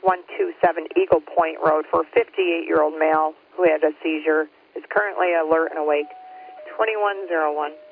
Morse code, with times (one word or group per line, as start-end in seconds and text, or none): none